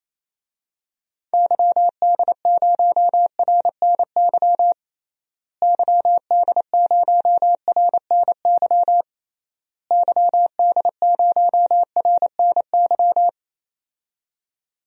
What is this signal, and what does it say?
Morse code, heard as a keyed tone, about 28 wpm, 700 Hz. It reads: YD0RNY YD0RNY YD0RNY
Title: DXCC-100 Call Signs - 6 Characters - Repeated 3x - 28wpm